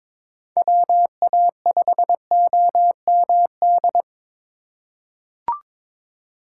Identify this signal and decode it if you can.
Morse code, heard as a keyed tone, about 22 wpm, 700 Hz. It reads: WA5OMD
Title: USA Call Signs - 6 Characters - 4x Character Spacing - 22wpm